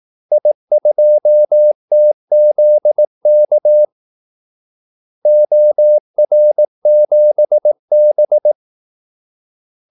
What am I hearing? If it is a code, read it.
Morse code, heard as a keyed tone, about 18 wpm, 600 Hz. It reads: I2TZK OR7B
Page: Kaldesignaler 18 wpm